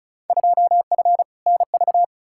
Morse code, heard as a keyed tone, about 34 wpm, 700 Hz. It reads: 2F NV
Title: Field Day Exchanges - 35wpm